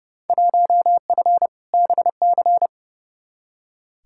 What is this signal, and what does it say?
Morse code, heard as a keyed tone, about 31 wpm, 700 Hz. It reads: 1F BC
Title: Field Day Exchanges Encore - 30wpm